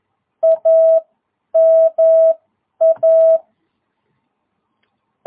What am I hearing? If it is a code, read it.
Morse code, heard as a keyed tone, about 11 wpm, spaced as for 9 wpm, 660 Hz. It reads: AMA
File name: jh8cbh / 2022cw